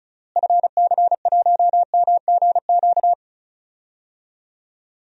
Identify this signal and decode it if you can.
Morse code, heard as a keyed tone, about 36 wpm, 700 Hz. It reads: FC1MGQ
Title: DXCC-100 Call Signs - 6 Characters - Repeated 2x - 35wpm